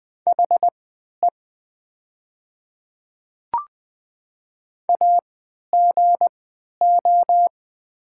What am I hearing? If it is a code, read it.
Morse code, heard as a keyed tone, about 21 wpm, spaced as for 12 wpm, 700 Hz. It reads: HE AGO
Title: 3x Character Spacing - Top 500 Words - 20wpm